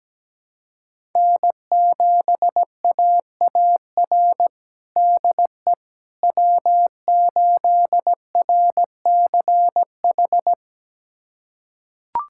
Morse code, heard as a keyed tone, about 17 wpm, 700 Hz. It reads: N7AAR DE W8RCH
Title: USA Callsign DE Callsign - 5 Characters - 17wpm